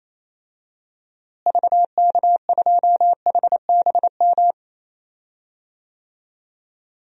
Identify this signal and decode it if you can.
Morse code, heard as a keyed tone, about 27 wpm, 700 Hz. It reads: VK2HBM
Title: DXCC-100 Call Signs - 6 Characters - Repeated 3x - 28wpm